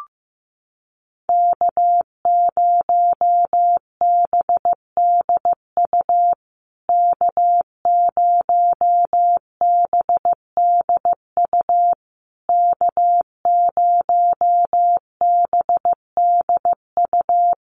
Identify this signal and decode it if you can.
Morse code, heard as a keyed tone, about 15 wpm, 700 Hz. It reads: K0BDU K0BDU K0BDU
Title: USA Call Signs Repeated 3x 15wpm